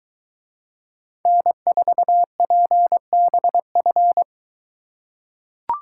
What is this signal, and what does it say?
Morse code, heard as a keyed tone, about 23 wpm, 700 Hz. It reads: N4PBF